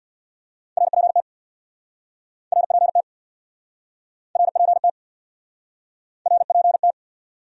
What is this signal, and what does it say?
Morse code, heard as a keyed tone, about 68 wpm, 700 Hz. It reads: RPT RPT RPT RPT